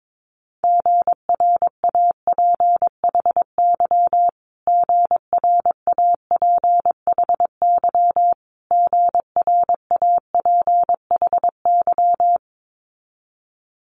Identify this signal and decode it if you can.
Morse code, heard as a keyed tone, about 22 wpm, 700 Hz. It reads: GRAPHY GRAPHY GRAPHY
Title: Multi-letter Phonemes Repeated 3x 22wpm